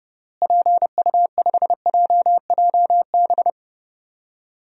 Morse code, heard as a keyed tone, about 32 wpm, 700 Hz. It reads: PU5JJB
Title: DXCC-100 Call Signs - 6 Characters - Repeated 3x - 30wpm